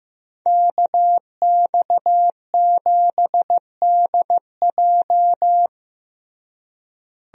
Morse code, heard as a keyed tone, about 15 wpm, 700 Hz. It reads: KX7DJ
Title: USA Call Signs - 5 Characters - Repeated 3x - 15wpm